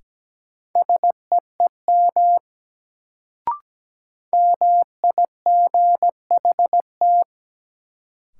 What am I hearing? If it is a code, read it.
Morse code, heard as a keyed tone, about 17 wpm, 700 Hz. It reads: SEEM MIGHT
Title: Sentences from Top 300 Words - New Word Drill - 17wpm